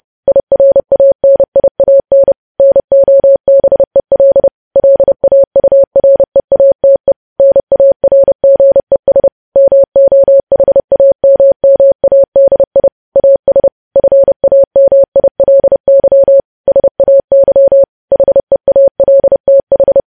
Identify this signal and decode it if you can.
Morse code, heard as a keyed tone, about 30 wpm, 550 Hz. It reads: IRANIAN NOBEL LAUREATE NARGES MOHAMMADI AS FAMILY SAY HEALTH